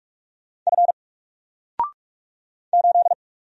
Morse code, heard as a keyed tone, about 44 wpm, 700 Hz. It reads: F 8
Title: Single Letter-Number Alt 45wpm